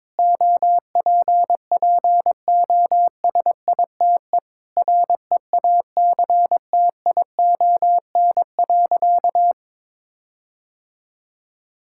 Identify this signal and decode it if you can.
Morse code, heard as a keyed tone, about 21 wpm, 700 Hz. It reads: OPPOSITE REACTION.
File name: practice / 22wpm / Sentences with 8-Word Phrases - 22wpm